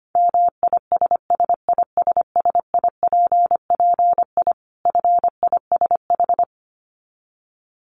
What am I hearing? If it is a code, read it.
Morse code, heard as a keyed tone, about 25 wpm, 700 Hz. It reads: MISSISSIPPI FISH
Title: Dit Challenge 25wpm